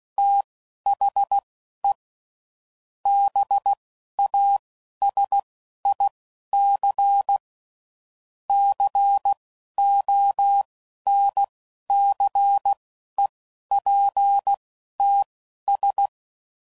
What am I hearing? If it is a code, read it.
Morse code, heard as a keyed tone, about 16 wpm, spaced as for 11 wpm, 800 Hz. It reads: THE BASIC CONCEPTS
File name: CW Code Groups @10WPM / Technical Subjects / Gates Intro